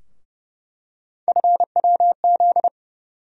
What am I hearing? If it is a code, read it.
Morse code, heard as a keyed tone, about 31 wpm, 700 Hz. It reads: FWZ
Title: USA Call Sign Postfixes - 3 Letters 30wpm